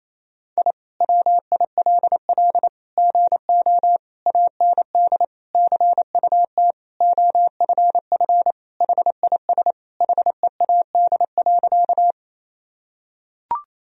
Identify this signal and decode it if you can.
Morse code, heard as a keyed tone, about 28 wpm, 700 Hz. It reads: I WILL GO AND CUT OFF HIS HEAD.
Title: Sentences from Top 1000 Words - Review Part 1 - 28wpm